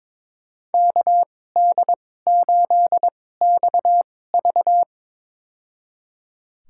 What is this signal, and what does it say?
Morse code, heard as a keyed tone, about 22 wpm, spaced as for 16 wpm, 700 Hz. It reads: KD8XV
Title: USA Call Signs - 5 Characters - 2x Character Spacing - 22wpm